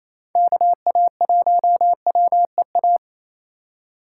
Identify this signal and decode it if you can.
Morse code, heard as a keyed tone, about 36 wpm, 700 Hz. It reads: KA1WEA